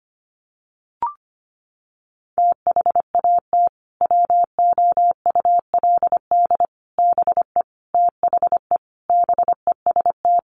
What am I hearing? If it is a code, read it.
Morse code, heard as a keyed tone, about 25 wpm, 700 Hz. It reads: THAT WOULD BE THE BEST
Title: Sentences from Top 400 Words 25wpm